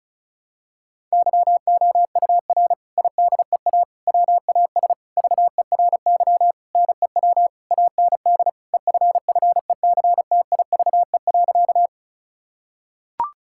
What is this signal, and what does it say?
Morse code, heard as a keyed tone, about 36 wpm, 700 Hz. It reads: YOUR IDEA WAS VERY NEW AND EFFECTIVE.
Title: Sentences from Top 1000 Words - Review Part 6 - 35wpm